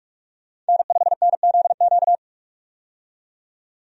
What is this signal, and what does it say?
Morse code, heard as a keyed tone, about 44 wpm, 700 Hz. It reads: N5NZQ
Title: USA Call Signs - 5 Characters - 45wpm